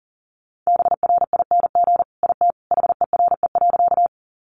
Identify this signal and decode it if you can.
Morse code, heard as a keyed tone, about 40 wpm, 700 Hz. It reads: BRING IT HERE.
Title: Sentences from Top 500 Words 40wpm